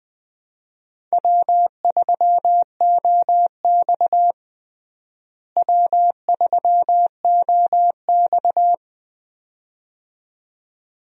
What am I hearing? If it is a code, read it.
Morse code, heard as a keyed tone, about 20 wpm, 700 Hz. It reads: W3OX W3OX
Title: USA Call Signs - 4 Characters - Repeated 2x - 20wpm